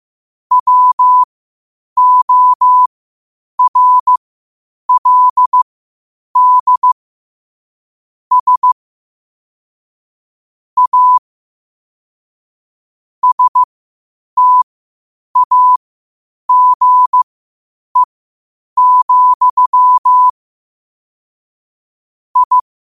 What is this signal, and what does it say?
Morse code, heard as a keyed tone, about 15 wpm, spaced as for 8 wpm, 1000 Hz. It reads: WORLD S A STAGE, I
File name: qotdfiles / 8wpm